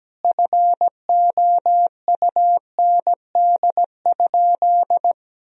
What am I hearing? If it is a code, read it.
Morse code, heard as a keyed tone, about 17 wpm, 700 Hz. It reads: FOUND?
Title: Sentences from Top 300 Words and Abbreviations 17wpm